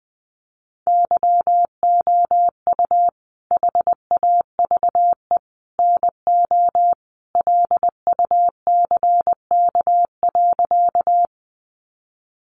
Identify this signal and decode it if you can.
Morse code, heard as a keyed tone, about 20 wpm, 700 Hz. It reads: YOU HAVE NO LUCK.
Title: Sentences from Top 1500 Words 20wpm